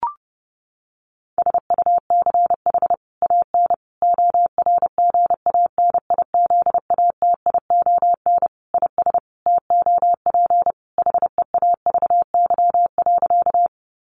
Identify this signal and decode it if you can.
Morse code, heard as a keyed tone, about 30 wpm, 700 Hz. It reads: SUCH AN ORGANIZATION IS TOP HEAVY.
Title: Sentences from Top 1100 Words 30wpm